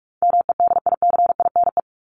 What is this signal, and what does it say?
Morse code, heard as a keyed tone, about 44 wpm, 700 Hz. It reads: MEDICINE
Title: Top 751-1000 Words 45wpm